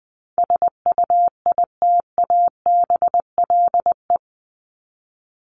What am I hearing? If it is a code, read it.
Morse code, heard as a keyed tone, about 20 wpm, 700 Hz. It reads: SUITABLE
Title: Top 1251-1500 Words 20wpm